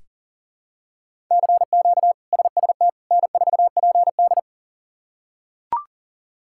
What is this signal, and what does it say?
Morse code, heard as a keyed tone, about 40 wpm, 700 Hz. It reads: CQ SST N4PD
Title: K1USN SST QSO Practice - 40wpm